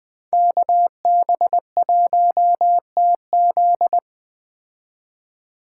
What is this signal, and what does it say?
Morse code, heard as a keyed tone, about 21 wpm, 700 Hz. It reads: KB1TZ
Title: Speed Racing - USA 2x2 Call Signs 17wpm